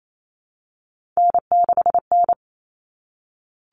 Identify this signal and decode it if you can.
Morse code, heard as a keyed tone, about 28 wpm, 700 Hz. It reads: N6N
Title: Common Cut Numbers - 3-Digit 28wpm